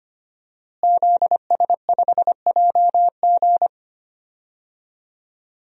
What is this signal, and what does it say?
Morse code, heard as a keyed tone, about 25 wpm, 700 Hz. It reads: ZS5JG
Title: DXCC-100 Call Signs - 5 Characters - Repeated 3x - 25wpm